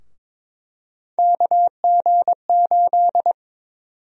Morse code, heard as a keyed tone, about 22 wpm, 700 Hz. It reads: KG8
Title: USA Call Sign Prefixes - 22wpm